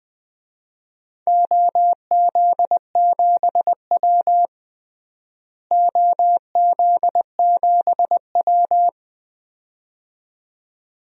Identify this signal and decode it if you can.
Morse code, heard as a keyed tone, about 21 wpm, 700 Hz. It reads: OZ7W OZ7W
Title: DXCC-100 Call Signs - 4 Characters - Repeated 2x - 20wpm